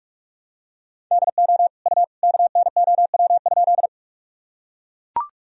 Morse code, heard as a keyed tone, about 43 wpm, 700 Hz. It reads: DO U KNOW?
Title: Sentences from Top 100 Words and Abbreviations 45wpm